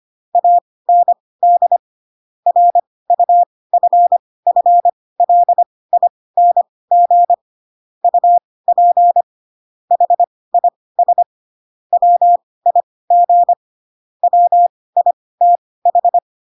Morse code, heard as a keyed tone, about 25 wpm, spaced as for 18 wpm, 700 Hz. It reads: AND RUFFLING UP HIS WIG WITH
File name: Pinocchio / Pinocchio ch1 12 25 ik1hge